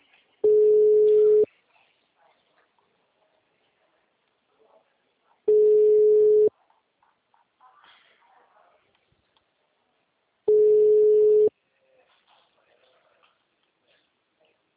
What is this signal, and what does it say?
Ringback tone (Europe, 425 Hz), with speech.
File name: out- -2005- - - .34